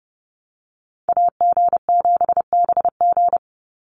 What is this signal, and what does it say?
Morse code, heard as a keyed tone, about 30 wpm, 700 Hz. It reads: AG7BG
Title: USA Call Signs 30wpm